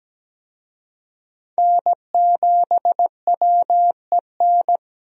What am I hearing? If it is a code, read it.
Morse code, heard as a keyed tone, about 17 wpm, 700 Hz. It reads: N7WEN
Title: USA Call Signs - 5 Characters - Repeated 3x - 17wpm